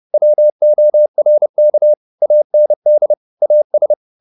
Morse code, heard as a keyed tone, about 30 wpm, 600 Hz. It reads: WORK AND AS
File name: the old man and the sea / the old man and the sea 1 30wpm